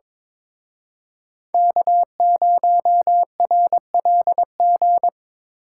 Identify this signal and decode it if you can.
Morse code, heard as a keyed tone, about 26 wpm, 700 Hz. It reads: K0RLG
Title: Speed Racing - USA 1x3 Call Signs 22wpm